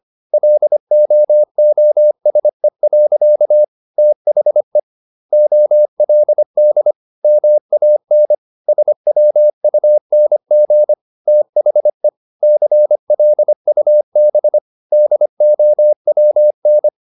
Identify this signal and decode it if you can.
Morse code, heard as a keyed tone, about 25 wpm, 600 Hz. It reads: LOOSE. THE OLD MAN SWUNG THE CLUB DOWN